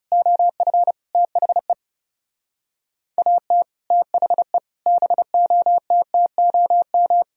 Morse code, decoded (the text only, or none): OF THE AT THE BOTTOM